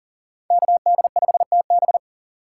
Morse code, heard as a keyed tone, about 37 wpm, 700 Hz. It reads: KD5TB
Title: USA Call Signs - 5 Characters - 4x Character Spacing - 40wpm